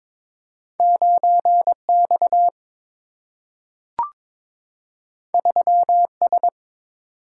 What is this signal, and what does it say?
Morse code, heard as a keyed tone, about 22 wpm, 700 Hz. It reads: 9X 3S
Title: Single-Letter Single-Number 22wpm